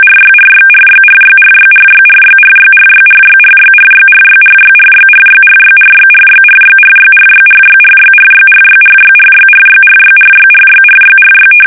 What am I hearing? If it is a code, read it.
RTTY, about 45.45 baud. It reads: RYRYRYRYRYRYRYRYRYRYRYRYRYRYRYRYRYRYRYRYRYRYRYRYRYRYRYRYRYRYRYRYRYRY
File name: rtty / RTTY Test 850